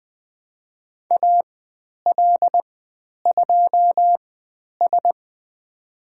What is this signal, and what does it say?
Morse code, heard as a keyed tone, about 20 wpm, spaced as for 10 wpm, 700 Hz. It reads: AL2S
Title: 102 Lesson 17g - Letter u - Call Signs 20wpm